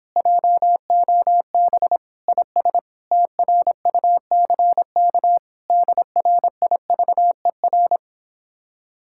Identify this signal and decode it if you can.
Morse code, heard as a keyed tone, about 26 wpm, 700 Hz. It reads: JOB IS TRUCK DRIVER